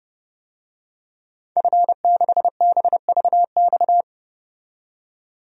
Morse code, heard as a keyed tone, about 30 wpm, 700 Hz. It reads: F6BVX
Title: DXCC-100 Call Signs - 5 Characters - 30wpm